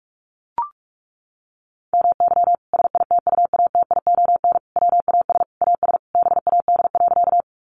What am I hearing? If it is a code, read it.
Morse code, heard as a keyed tone, about 44 wpm, 700 Hz. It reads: MY SITUATION WAS AS BAD.
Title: Sentences from Top 1200 Words 45wpm